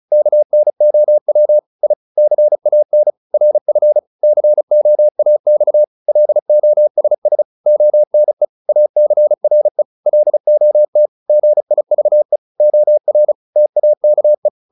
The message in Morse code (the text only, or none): KNOW I CAN RF COAX LOSS ONE ACRE LOT GIVE OR TAKE